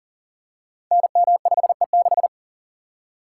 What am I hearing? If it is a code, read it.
Morse code, heard as a keyed tone, about 38 wpm, 700 Hz. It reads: NM5E6
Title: Code Groups - 5 Letter:Numbers - 40wpm